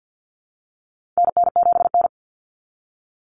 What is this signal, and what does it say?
Morse code, heard as a keyed tone, about 50 wpm, 700 Hz. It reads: NN7N
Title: Common Cut Numbers - 4-Digit 50wpm